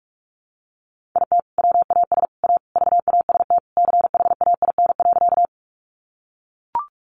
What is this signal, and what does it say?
Morse code, heard as a keyed tone, about 44 wpm, 700 Hz. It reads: IT WAS A VAST CHAIN.
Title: Sentences from Top 1500 Words 45wpm